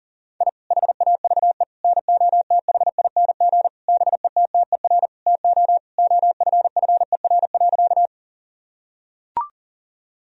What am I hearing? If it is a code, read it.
Morse code, heard as a keyed tone, about 40 wpm, 700 Hz. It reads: I HAVE NOTHING BETTER TO OFFER.
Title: Sentences from Top 1000 Words - Review Part 4 - 40wpm